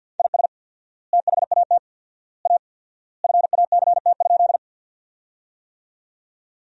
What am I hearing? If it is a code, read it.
Morse code, heard as a keyed tone, about 47 wpm, 700 Hz. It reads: IS THAT A FACT?
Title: 4x Word-Spacing - Sentences from Top 500 Words - 50wpm